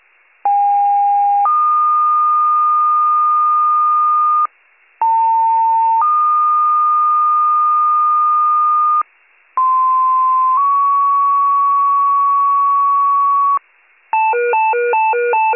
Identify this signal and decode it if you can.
Two-tone page, heard as a one-second tone, then a three-second tone, about 810 then 1220 Hz, about 900 then 1220 Hz, about 1040 then 1120 Hz. It is heard from 0.4 to 13.6 s.